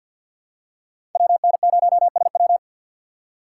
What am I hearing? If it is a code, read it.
Morse code, heard as a keyed tone, about 48 wpm, 700 Hz. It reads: WN0SW
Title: USA Call Signs - 5 Characters - 4x Character Spacing - 50wpm